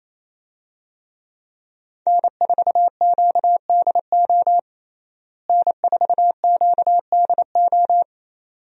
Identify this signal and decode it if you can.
Morse code, heard as a keyed tone, about 28 wpm, 700 Hz. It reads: N4QDO N4QDO
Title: USA Call Signs - 5 Characters - Repeated 2x - 28wpm